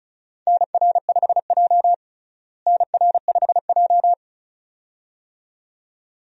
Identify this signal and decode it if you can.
Morse code, heard as a keyed tone, about 36 wpm, 700 Hz. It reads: NR5J NR5J
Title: USA Call Signs - 4 Characters - Repeated 2x - 35wpm